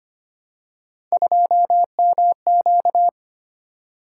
Morse code, heard as a keyed tone, about 25 wpm, 700 Hz. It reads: 2MQ
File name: Three Letter-Number Combinations - 25wpm